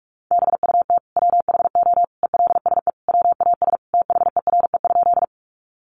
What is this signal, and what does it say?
Morse code, heard as a keyed tone, about 44 wpm, 700 Hz. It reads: BUT WHO ELSE WAS THERE?
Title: Sentences from Top 800 Words 45wpm